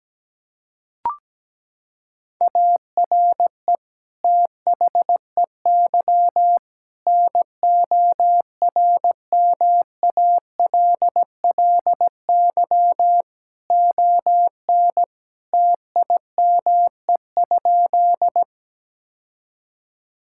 Morse code, heard as a keyed tone, about 17 wpm, 700 Hz. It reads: ARE THEY NORMALLY ON TIME?
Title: Sentences from Top 1000 Words - Review Part 6 - 17wpm